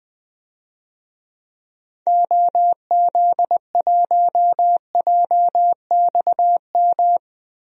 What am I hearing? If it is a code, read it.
Morse code, heard as a keyed tone, about 21 wpm, 700 Hz. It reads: OZ1JXM
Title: DXCC-100 Call Signs - 6 Characters - Repeated 2x - 20wpm